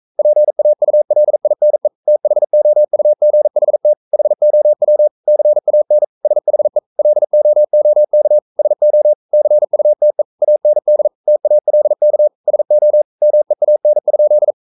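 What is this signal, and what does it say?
Morse code, heard as a keyed tone, about 42 wpm, 600 Hz. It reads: PAULINE THOUGHT HOW CAN SHE LOOK SO CUTE AND TALK SO MEAN?
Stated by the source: Morse code sending of an English novel at 42 wpm.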